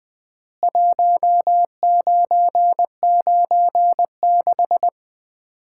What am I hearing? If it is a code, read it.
Morse code, heard as a keyed tone, about 20 wpm, 700 Hz. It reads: 1996
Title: Common QSO Phrases - 3x Word-Spacing - 20wpm